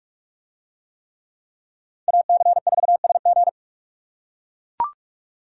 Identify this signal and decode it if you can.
Morse code, heard as a keyed tone, about 44 wpm, 700 Hz. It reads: AC4SG